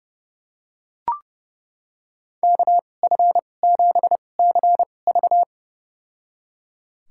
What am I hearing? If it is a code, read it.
Morse code, heard as a keyed tone, about 32 wpm, spaced as for 22 wpm, 700 Hz. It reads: KF7CV
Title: USA Call Signs - 5 Characters - 2x Character Spacing - 30wpm